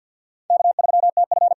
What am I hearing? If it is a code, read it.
Morse code, heard as a keyed tone, about 51 wpm, 700 Hz. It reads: K3TF
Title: K1USN SST QSO Practice - 50wpm